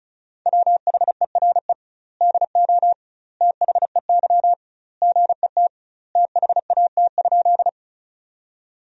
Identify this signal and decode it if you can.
Morse code, heard as a keyed tone, about 36 wpm, 700 Hz. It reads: WHERE DO THEY GET THAT?
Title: 2x Word-Spacing - Sentences from Top 200 Words - 35wpm